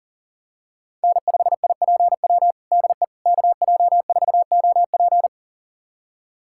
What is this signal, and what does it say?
Morse code, heard as a keyed tone, about 37 wpm, 700 Hz. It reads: N5IPW DE KJ4OP
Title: USA Callsign DE Callsign - 5 Characters - 40wpm